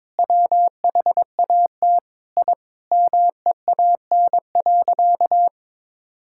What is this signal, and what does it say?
Morse code, heard as a keyed tone, about 21 wpm, 700 Hz. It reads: WHAT I MEAN.
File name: practice / 22wpm / Sentences with 4-Word Phrases - 22wpm